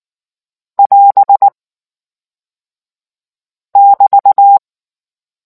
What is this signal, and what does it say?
Morse code, heard as a keyed tone, about 19 wpm, 800 Hz. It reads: <AS> <BT>